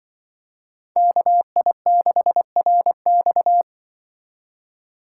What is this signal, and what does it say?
Morse code, heard as a keyed tone, about 24 wpm, 700 Hz. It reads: KI6RX
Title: Speed Racing - USA 2x2 Call Signs 20wpm